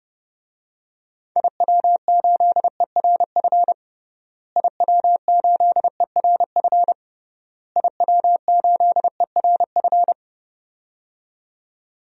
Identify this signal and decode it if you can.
Morse code, heard as a keyed tone, about 30 wpm, 700 Hz. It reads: IW8ERF IW8ERF IW8ERF
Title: DXCC-100 Call Signs - 6 Characters - Repeated 3x - 30wpm